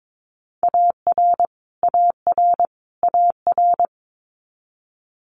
Morse code, heard as a keyed tone, about 22 wpm, 700 Hz. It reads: AR AR AR
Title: Multi-letter Phonemes Repeated 3x 22wpm